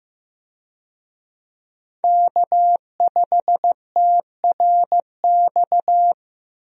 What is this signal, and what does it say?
Morse code, heard as a keyed tone, about 15 wpm, 700 Hz. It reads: K5TRX
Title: USA Call Signs - 5 Characters - 15wpm